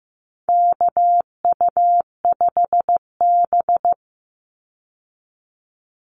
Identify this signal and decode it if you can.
Morse code, heard as a keyed tone, about 15 wpm, 700 Hz. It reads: KU5B
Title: Call Signs Repeated 2x - 15wpm